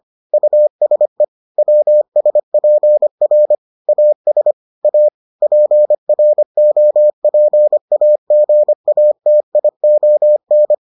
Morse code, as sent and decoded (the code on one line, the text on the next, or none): ..- ... / .-- ... .--. .-. / .- ... / .- / .--. .-. --- .--. .- --. .- - .. --- -.
USE WSPR AS A PROPAGATION